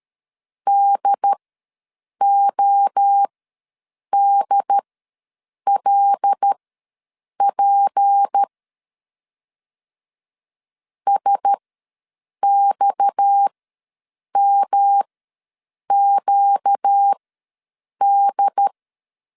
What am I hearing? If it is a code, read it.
Morse code, heard as a keyed tone, about 13 wpm, spaced as for 7 wpm, 800 Hz. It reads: DODLP SXMQD
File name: LEZIONI / 6 PKX0 / 6 PKX ws 12wpm 3106